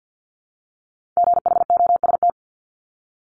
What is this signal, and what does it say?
Morse code, heard as a keyed tone, about 50 wpm, 690 Hz. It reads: GHOST
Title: Multi-letter Phonemes 50wpm